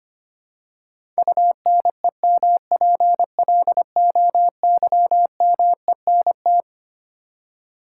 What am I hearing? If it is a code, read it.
Morse code, heard as a keyed tone, about 25 wpm, 700 Hz. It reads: UNEMPLOYMENT
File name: Twelve Letter Words - 25wpm